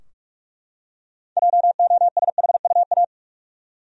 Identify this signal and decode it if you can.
Morse code, heard as a keyed tone, about 43 wpm, 700 Hz. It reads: JOSHUA
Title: USA Common Names - 45wpm